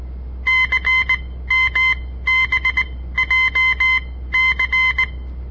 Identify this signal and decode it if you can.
Morse code, heard as a keyed tone, about 19 wpm, spaced as for 14 wpm, 2050 Hz. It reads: CMBJC
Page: On the air Morse code practice from the Southern Montana Amatuer Radio Association; all code, no read back